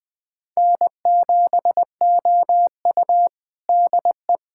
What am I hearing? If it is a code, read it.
Morse code, heard as a keyed tone, about 20 wpm, 700 Hz. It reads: N7OU DE
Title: USA Callsign DE Callsign - 4 Characters - 20wpm